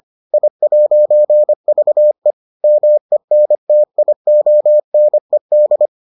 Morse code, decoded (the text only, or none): I'VE MENTIONED